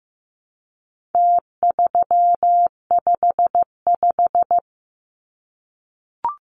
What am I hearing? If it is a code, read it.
Morse code, heard as a keyed tone, about 15 wpm, 700 Hz. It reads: T355
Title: Numbers - Contest Style - 4-Digit 15wpm